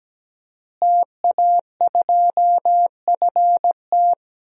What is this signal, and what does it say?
Morse code, heard as a keyed tone, about 17 wpm, 700 Hz. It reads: TA2FT
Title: DXCC-100 Call Signs - 5 Characters - Repeated 2x - 17wpm